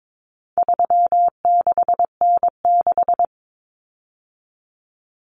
Morse code, heard as a keyed tone, about 22 wpm, 700 Hz. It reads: 36N6
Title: Common Cut Numbers - 4-Digit 22wpm